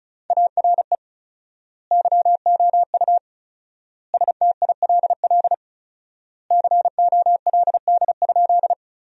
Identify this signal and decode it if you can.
Morse code, heard as a keyed tone, about 36 wpm, 700 Hz. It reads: ARE YOU STILL COLD?
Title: 4x Word-Spacing - Sentences from Top 400 Words - 35wpm